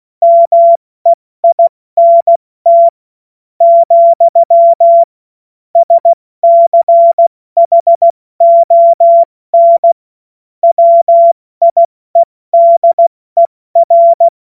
Morse code, heard as a keyed tone, about 16 wpm, spaced as for 14 wpm, 680 Hz. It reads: MEINT, SCHON WIEDER
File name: Morsekurs / Witze / Witz 01. November 2022 70 F80